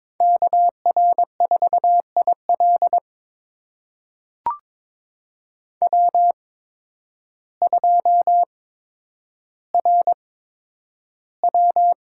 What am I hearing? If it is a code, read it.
Morse code, heard as a keyed tone, about 22 wpm, spaced as for 10 wpm, 700 Hz. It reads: KR4IL W2RW
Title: USA Call Signs - 5 Characters - 8x Character Spacing - 22wpm